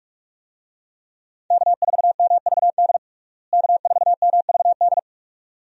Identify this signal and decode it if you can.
Morse code, heard as a keyed tone, about 47 wpm, 700 Hz. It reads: K4MVD K4MVD
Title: USA Call Signs - 5 Characters - Repeated 2x - 45wpm